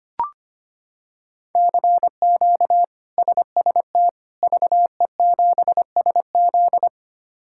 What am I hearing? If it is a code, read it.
Morse code, heard as a keyed tone, about 25 wpm, 700 Hz. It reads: CQ SST VE7SZ
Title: K1USN SST QSO Practice - 25wpm